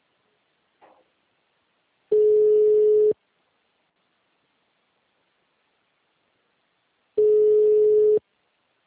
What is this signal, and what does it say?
Ringback tone (Europe, 425 Hz).
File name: out- -2009- - - .64189